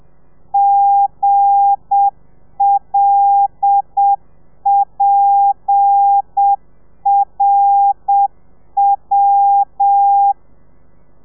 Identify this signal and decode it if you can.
Morse code, heard as a keyed tone, about 7 wpm, 800 Hz. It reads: GLPRW
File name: telegrafia / i2tfj / Lez8 30